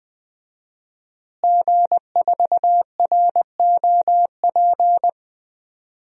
Morse code, heard as a keyed tone, about 20 wpm, 700 Hz. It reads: G4ROP
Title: DXCC-100 Call Signs - 5 Characters - Repeated 2x - 20wpm